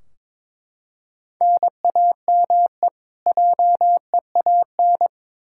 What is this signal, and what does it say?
Morse code, heard as a keyed tone, about 22 wpm, 700 Hz. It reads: NAME JEAN
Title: Practice QSO - Exchange 1 - Beginning - 22wpm